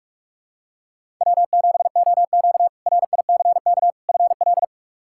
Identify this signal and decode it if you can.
Morse code, heard as a keyed tone, about 42 wpm, 700 Hz. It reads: W7OQ RICK FL
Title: K1USN SST QSO Practice - 45wpm